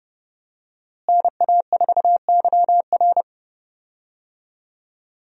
Morse code, heard as a keyed tone, about 30 wpm, 700 Hz. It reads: NA4YR
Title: USA Call Signs - 5 Characters - Repeated 3x - 30wpm